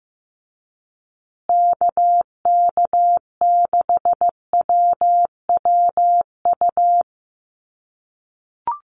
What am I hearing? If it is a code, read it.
Morse code, heard as a keyed tone, about 15 wpm, 700 Hz. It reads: KK6WWU
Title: USA Call Signs Repeated 3x 15wpm